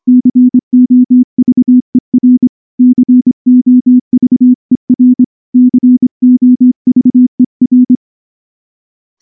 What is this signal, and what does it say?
Morse code, heard as a keyed tone, about 25 wpm, 250 Hz. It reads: COVER COVER COVER
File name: morse-training-25WPM